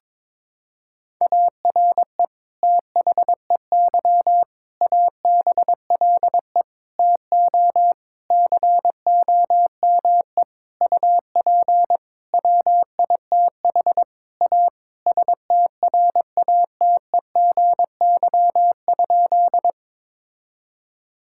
Morse code, heard as a keyed tone, about 22 wpm, 700 Hz. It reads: ARE THEY ABLE TO COME UP WITH A STRATEGY?
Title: Sentences from Top 1000 Words - Review Part 5 - 22wpm